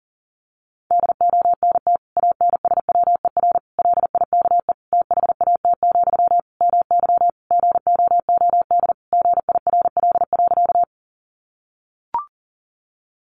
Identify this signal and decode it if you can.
Morse code, heard as a keyed tone, about 40 wpm, 700 Hz. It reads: DONT ANSWER LIKE THAT, MY GOOD GIRL.
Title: Sentences from Top 400 Words 40wpm